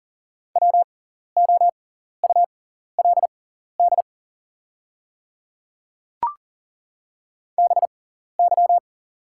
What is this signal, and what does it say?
Morse code, heard as a keyed tone, about 40 wpm, spaced as for 14 wpm, 690 Hz. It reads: WOULD BY
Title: 6x Character Spacing - Top 100 Words - 40wpm